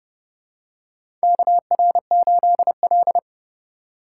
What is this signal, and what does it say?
Morse code, heard as a keyed tone, about 30 wpm, 700 Hz. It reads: KR8L
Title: K1USN SST QSO Practice - 30wpm